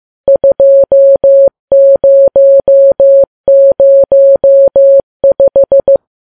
Morse code, heard as a keyed tone, about 15 wpm, 550 Hz. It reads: 2005
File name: Files Web tls / Morse on9fv / morse15kps